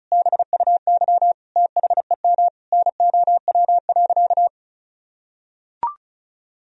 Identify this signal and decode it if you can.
Morse code, heard as a keyed tone, about 36 wpm, 700 Hz. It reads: BUY THEM NOW.
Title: Sentences from Top 1000 Words - Review Part 4 - 35wpm